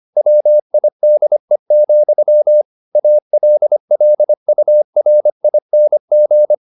Morse code, heard as a keyed tone, about 25 wpm, 600 Hz. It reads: WIDE, ALLURING